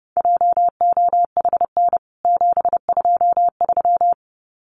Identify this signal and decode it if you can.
Morse code, heard as a keyed tone, about 30 wpm, 700 Hz. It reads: JOHN 723
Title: CWT Exchange 30wpm